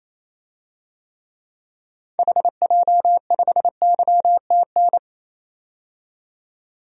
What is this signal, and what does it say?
Morse code, heard as a keyed tone, about 28 wpm, 700 Hz. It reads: HJ5YTN
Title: DXCC-100 Call Signs - 6 Characters - 28wpm